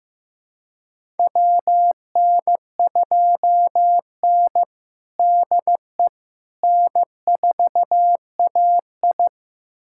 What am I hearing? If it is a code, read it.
Morse code, heard as a keyed tone, about 15 wpm, 700 Hz. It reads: WN2N DE N4AI
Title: USA Callsign DE Callsign - 4 Characters - 15wpm